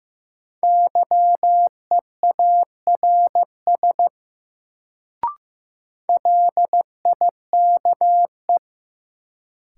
Morse code, heard as a keyed tone, about 15 wpm, 700 Hz. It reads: YEARS LIKE
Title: Sets of 2 Words - New Word Drill - 15wpm